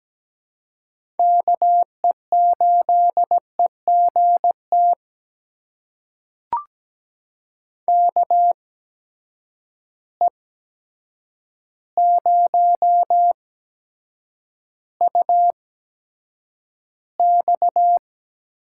Morse code, heard as a keyed tone, about 17 wpm, spaced as for 8 wpm, 700 Hz. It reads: KE8EGT KE0UX
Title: USA Call Signs - 6 Characters - 8x Character Spacing - 17wpm